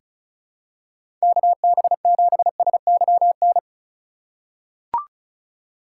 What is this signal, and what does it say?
Morse code, heard as a keyed tone, about 36 wpm, 700 Hz. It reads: KB7SYN
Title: USA Call Signs - 6 Characters - 6x Character Spacing - 35wpm